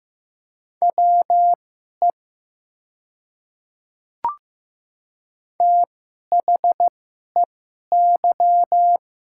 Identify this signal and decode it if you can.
Morse code, heard as a keyed tone, about 15 wpm, spaced as for 11 wpm, 700 Hz. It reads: WE THEY
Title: 2x Character Spacing - Top 100 Words - 15wpm